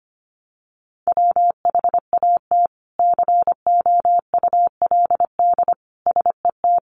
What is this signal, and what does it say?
Morse code, heard as a keyed tone, about 25 wpm, 700 Hz. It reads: WHAT COULD SET
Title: Sentences from Top 200 Words 25wpm